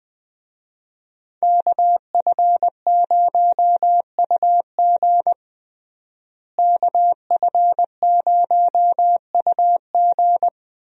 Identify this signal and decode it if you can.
Morse code, heard as a keyed tone, about 20 wpm, 700 Hz. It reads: KF0UG KF0UG
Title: USA Call Signs - 5 Characters - Repeated 3x - 20wpm